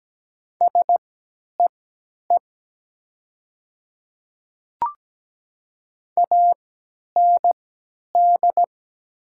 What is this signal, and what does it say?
Morse code, heard as a keyed tone, about 17 wpm, spaced as for 10 wpm, 690 Hz. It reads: SEE AND
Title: 3x Character Spacing - Top 100 Words - 17wpm